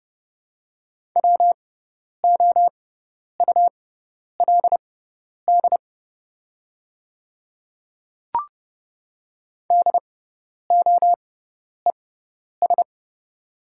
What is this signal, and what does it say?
Morse code, heard as a keyed tone, about 30 wpm, spaced as for 10 wpm, 700 Hz. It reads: WOULD DOES
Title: 6x Character Spacing - Top 300 Words - 30wpm